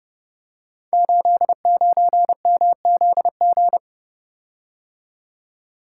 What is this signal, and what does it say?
Morse code, heard as a keyed tone, about 28 wpm, 700 Hz. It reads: 89MZG
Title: Code Groups - 5 Letter:Numbers - 30wpm